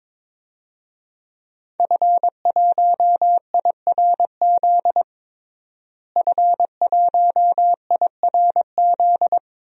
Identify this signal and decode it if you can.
Morse code, heard as a keyed tone, about 21 wpm, 700 Hz. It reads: F1IRZ F1IRZ
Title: DXCC-100 Call Signs - 5 Characters - Repeated 3x - 22wpm